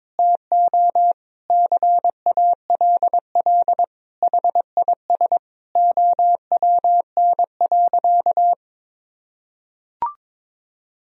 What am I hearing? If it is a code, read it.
Morse code, heard as a keyed tone, about 21 wpm, 700 Hz. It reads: TO CALL HIS OWN.
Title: Sentences from Top 1000 Words - Review Part 2 - 22wpm